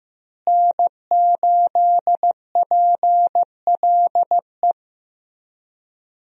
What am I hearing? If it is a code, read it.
Morse code, heard as a keyed tone, about 15 wpm, 700 Hz. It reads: N8PLE
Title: USA Call Signs - 5 Characters - Repeated 3x - 15wpm